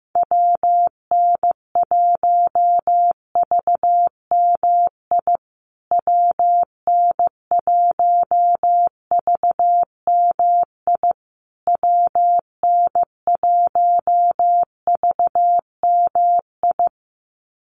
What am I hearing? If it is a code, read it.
Morse code, heard as a keyed tone, about 15 wpm, 700 Hz. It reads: WN1VMI WN1VMI WN1VMI
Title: USA Call Signs Repeated 3x 15wpm